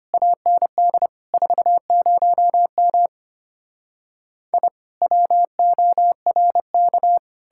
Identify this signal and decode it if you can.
Morse code, heard as a keyed tone, about 33 wpm, 700 Hz. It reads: AND 40M I WORK